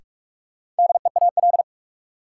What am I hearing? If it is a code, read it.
Morse code, heard as a keyed tone, about 46 wpm, 700 Hz. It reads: DEAL
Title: Sentences from Top 800 Words - New Word Drill - 45wpm